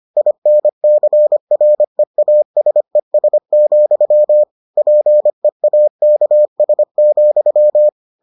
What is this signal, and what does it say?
Morse code, heard as a keyed tone, about 25 wpm, 600 Hz. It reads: INCREASES, PEAKS,